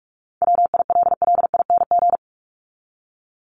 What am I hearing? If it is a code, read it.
Morse code, heard as a keyed tone, about 44 wpm, 700 Hz. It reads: WILLING
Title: Top 751-1000 Words 45wpm